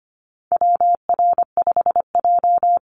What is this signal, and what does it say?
Morse code, heard as a keyed tone, about 25 wpm, 700 Hz. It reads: WR5J
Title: Call Signs - 25wpm